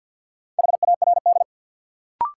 Morse code, heard as a keyed tone, about 49 wpm, 700 Hz. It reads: HARD
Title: Sentences from Top 300 Words - New Word Drill - 50wpm